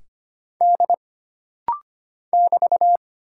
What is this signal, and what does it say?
Morse code, heard as a keyed tone, about 25 wpm, 700 Hz. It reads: D <BT>
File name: Single Character (Letter:Number:Punctuation:Prosign) - Rapid-Fire - 25wpm